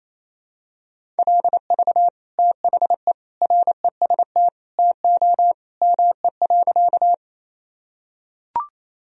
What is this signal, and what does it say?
Morse code, heard as a keyed tone, about 28 wpm, 700 Hz. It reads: LV THE REST TO ME.
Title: Sentences from Top 500 Words and Abbreviations 28wpm